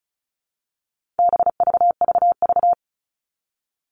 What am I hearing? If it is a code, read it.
Morse code, heard as a keyed tone, about 34 wpm, 700 Hz. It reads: BVVV
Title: All Cut Numbers - 4-Digit 35wpm